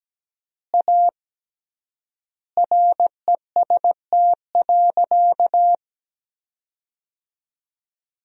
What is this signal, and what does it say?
Morse code, heard as a keyed tone, about 17 wpm, 700 Hz. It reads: A REST.
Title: 3x Word-Spacing - Sentences from Top 500 Words - 17wpm